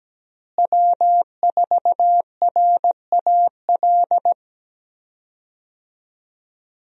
Morse code, heard as a keyed tone, about 17 wpm, 700 Hz. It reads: W4RAL
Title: USA Call Signs - 5 Characters - Repeated 2x - 17wpm